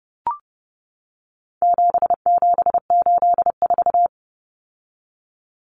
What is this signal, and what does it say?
Morse code, heard as a keyed tone, about 30 wpm, 700 Hz. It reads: 7784
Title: Common Cut Numbers - 4-Digit 30wpm